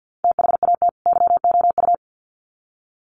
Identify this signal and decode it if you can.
Morse code, heard as a keyed tone, about 50 wpm, 700 Hz. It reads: THAT YOU
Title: Sets of 2 Words 50wpm